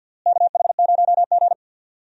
Morse code, heard as a keyed tone, about 52 wpm, 700 Hz. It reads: KH0G
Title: USA Call Signs - 4 Characters - Repeated 3x - 50wpm